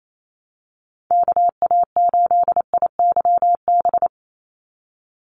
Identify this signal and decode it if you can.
Morse code, heard as a keyed tone, about 28 wpm, 700 Hz. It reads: KA8IYB
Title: USA Call Signs 28wpm